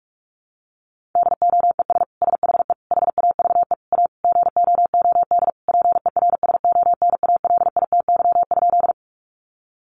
Morse code, heard as a keyed tone, about 44 wpm, 700 Hz. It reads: DOES SHE HAVE A GOOD PERSONALITY?
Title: Sentences from Top 1400 Words 45wpm